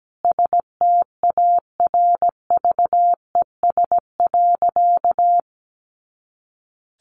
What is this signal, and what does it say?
Morse code, heard as a keyed tone, about 17 wpm, 700 Hz. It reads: STARVES.
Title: A Sound Of Thunder 17wpm